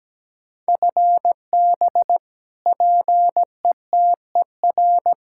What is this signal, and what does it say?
Morse code, heard as a keyed tone, about 17 wpm, 700 Hz. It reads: FB PETER
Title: Practice QSO - Exchange 3 - More Info - 17wpm